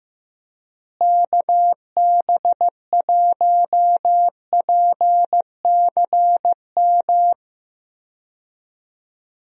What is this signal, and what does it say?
Morse code, heard as a keyed tone, about 15 wpm, 700 Hz. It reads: KB1PCM
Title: USA Call Signs - 6 Characters - Repeated 3x - 15wpm